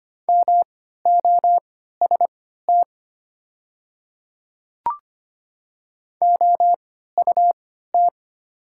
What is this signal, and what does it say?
Morse code, heard as a keyed tone, about 25 wpm, spaced as for 14 wpm, 700 Hz. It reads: MOST OUT
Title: 3x Character Spacing - Top 200 Words - 25wpm